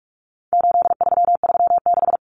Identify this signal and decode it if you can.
Morse code, heard as a keyed tone, about 44 wpm, 700 Hz. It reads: 8336
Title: Common Cut Numbers - 4-Digit 45wpm